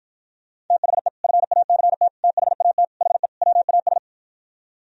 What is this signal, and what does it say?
Morse code, heard as a keyed tone, about 52 wpm, 700 Hz. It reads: THE FACT THAT HE WAS